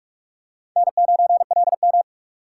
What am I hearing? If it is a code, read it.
Morse code, heard as a keyed tone, about 44 wpm, 700 Hz. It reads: N9LM